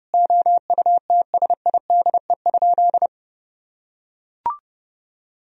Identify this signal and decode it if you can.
Morse code, heard as a keyed tone, about 32 wpm, 700 Hz. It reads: OUTSIDE?
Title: Sentences from Top 1000 Words - Review Part 4 - 30wpm